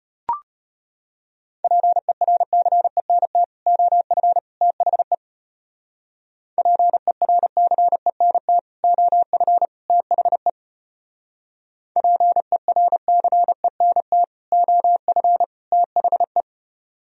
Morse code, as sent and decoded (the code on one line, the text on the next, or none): .--. .-. -.-. -. - / --- ..-. / - .... / .--. .-. -.-. -. - / --- ..-. / - .... / .--. .-. -.-. -. - / --- ..-. / - .... .
PERCENT OF THE PERCENT OF THE PERCENT OF THE